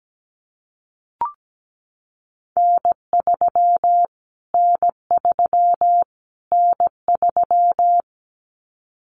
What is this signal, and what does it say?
Morse code, heard as a keyed tone, about 17 wpm, 700 Hz. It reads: N3 N3 N3
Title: Callsign Prefixes for Top 25 DXCC - Repeated 3x - 17wpm